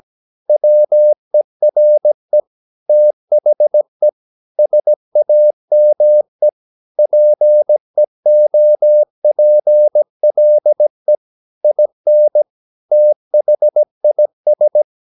Morse code, heard as a keyed tone, about 17 wpm, 600 Hz. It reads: WERE THE SAME PEOPLE IN THIS
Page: Romanen Never af Ken Follet på engelsk.